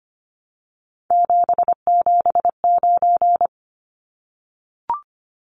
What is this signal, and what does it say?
Morse code, heard as a keyed tone, about 25 wpm, 700 Hz. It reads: 779
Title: Numbers - 3-Digit 25wpm